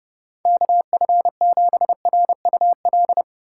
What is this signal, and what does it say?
Morse code, heard as a keyed tone, about 30 wpm, 700 Hz. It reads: KF7RUL
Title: USA Call Signs - 6 Characters - 6x Character Spacing - 30wpm